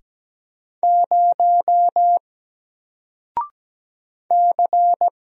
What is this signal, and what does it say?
Morse code, heard as a keyed tone, about 17 wpm, 700 Hz. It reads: 0 C
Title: Single Letter-Number 17wpm